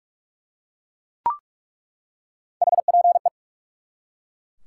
Morse code, heard as a keyed tone, about 45 wpm, 700 Hz. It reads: HPE
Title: Top 90 QSO Elements 45wpm